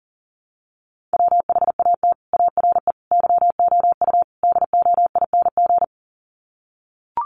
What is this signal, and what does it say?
Morse code, heard as a keyed tone, about 40 wpm, 700 Hz. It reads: WHAT ARE YOU DOING
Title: Sets of 4 Words 40wpm